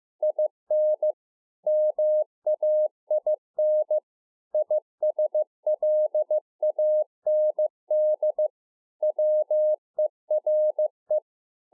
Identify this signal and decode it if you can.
Morse code, heard as a keyed tone, about 15 wpm, 610 Hz. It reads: IN MAIN ISLAND WERE